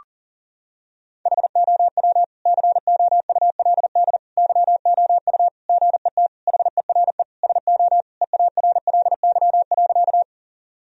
Morse code, heard as a keyed tone, about 40 wpm, 700 Hz. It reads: HOW COULD YOU GET HERE SO EARLY.
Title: Sentences from Top 1000 Words - Review Part 2 - 40wpm